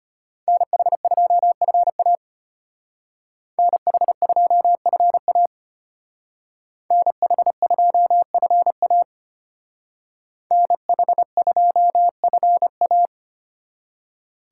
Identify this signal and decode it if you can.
Morse code, heard as a keyed tone, about 32 wpm, 700 Hz. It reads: NH2FA NH2FA NH2FA NH2FA